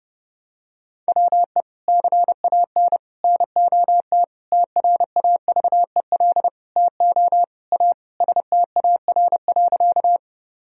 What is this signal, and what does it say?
Morse code, heard as a keyed tone, about 30 wpm, 700 Hz. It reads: WE CAN NOT TRAVEL TO A STAR.
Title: Sentences from Top 1000 Words - Review Part 2 - 30wpm